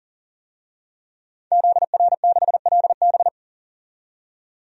Morse code, heard as a keyed tone, about 37 wpm, 700 Hz. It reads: ZR6LB
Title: DXCC-100 Call Signs - 5 Characters - 40wpm